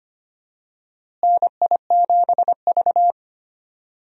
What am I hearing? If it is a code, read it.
Morse code, heard as a keyed tone, about 25 wpm, 700 Hz. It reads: NI7V
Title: USA Call Signs - 4 Characters - Repeated 3x - 25wpm